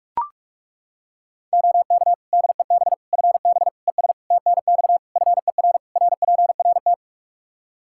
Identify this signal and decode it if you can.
Morse code, heard as a keyed tone, about 41 wpm, 700 Hz. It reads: OK DEB FB ES TNX FER RPRT
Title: Practice QSO - Exchange 2 - Info - 45wpm